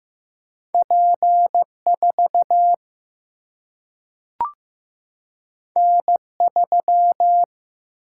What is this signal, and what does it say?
Morse code, heard as a keyed tone, about 15 wpm, 700 Hz. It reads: P 4 N 3
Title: Single-Letter Single-Number 15wpm